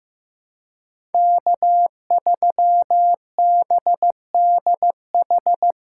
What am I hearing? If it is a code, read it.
Morse code, heard as a keyed tone, about 15 wpm, 700 Hz. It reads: K3BDH
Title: USA Call Signs - 5 Characters - 4x Character Spacing - 15wpm